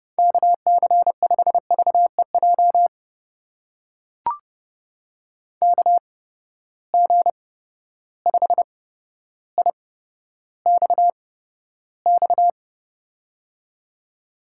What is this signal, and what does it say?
Morse code, heard as a keyed tone, about 30 wpm, spaced as for 13 wpm, 700 Hz. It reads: KC5VEJ KG5IXX
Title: USA Call Signs - 6 Characters - 8x Character Spacing - 30wpm